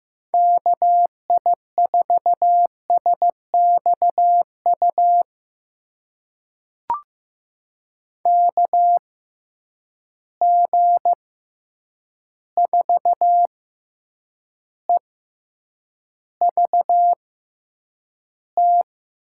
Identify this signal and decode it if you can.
Morse code, heard as a keyed tone, about 15 wpm, spaced as for 8 wpm, 700 Hz. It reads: KI4SXU KG4EVT
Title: USA Call Signs - 6x Character Spacing - 15wpm